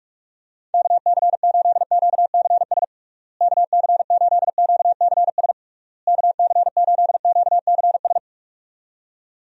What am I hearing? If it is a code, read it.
Morse code, heard as a keyed tone, about 46 wpm, 700 Hz. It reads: KC8QCS KC8QCS KC8QCS
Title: USA Call Signs - 6 Characters - Repeated 3x - 45wpm